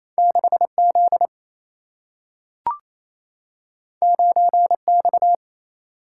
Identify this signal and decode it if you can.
Morse code, heard as a keyed tone, about 28 wpm, 700 Hz. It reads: 6Z 9X
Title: Single-Letter Single-Number 28wpm